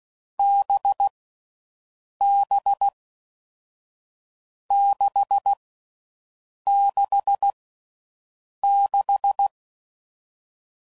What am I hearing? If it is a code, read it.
Morse code, heard as a keyed tone, about 16 wpm, 800 Hz. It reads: B B 6 6 6
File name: Combinations / Difficult Combos / NDB NDB6